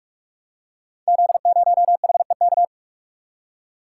Morse code, heard as a keyed tone, about 46 wpm, 700 Hz. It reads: Z0HEK
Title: Code Groups - 5 Letter:Numbers - 45wpm